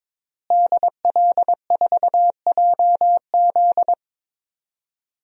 Morse code, heard as a keyed tone, about 21 wpm, 700 Hz. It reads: DL4JZ
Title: DXCC-100 Call Signs - 5 Characters - Repeated 3x - 22wpm